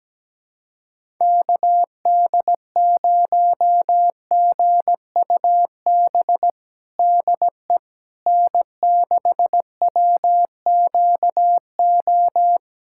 Morse code, heard as a keyed tone, about 17 wpm, 700 Hz. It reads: KD0GUB DE N6WQO
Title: USA Callsign DE Callsign 17wpm